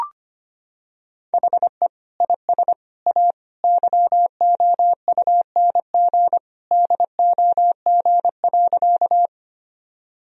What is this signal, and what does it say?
Morse code, heard as a keyed tone, about 25 wpm, 700 Hz. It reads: HE IS A YOUNG DOG.
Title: Sentences from Top 1000 Words - Review Part 2 - 25wpm